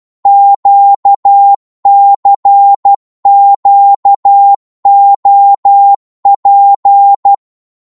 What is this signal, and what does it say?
Morse code, heard as a keyed tone, about 12 wpm, 800 Hz. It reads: QCQOP
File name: Morse / ltr-12wpm